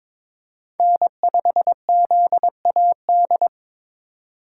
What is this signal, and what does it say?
Morse code, heard as a keyed tone, about 26 wpm, 700 Hz. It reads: N5ZAD
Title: Speed Racing - USA 1x3 Call Signs 22wpm